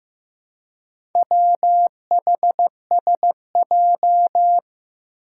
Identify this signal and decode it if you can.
Morse code, heard as a keyed tone, about 15 wpm, 700 Hz. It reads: WHSJ
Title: Four Challenging Letter-Number Combinations 15wpm